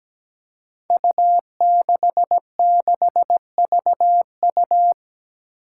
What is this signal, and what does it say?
Morse code, heard as a keyed tone, about 17 wpm, 700 Hz. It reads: U66VU
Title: Five Challenging Letter-Number Combinations 17wpm